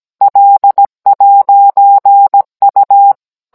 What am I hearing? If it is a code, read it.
Morse code, heard as a keyed tone, about 17 wpm, 800 Hz. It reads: L'U